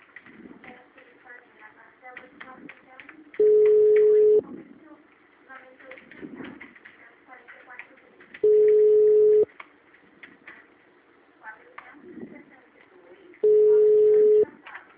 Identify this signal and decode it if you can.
Ringback tone (Europe, 425 Hz), with speech.
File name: out- -2055- - - .42371